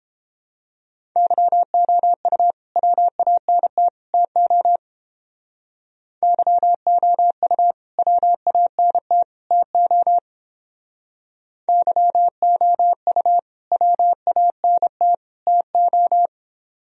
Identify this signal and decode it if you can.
Morse code, heard as a keyed tone, about 29 wpm, 700 Hz. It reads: YOU WANT TO YOU WANT TO YOU WANT TO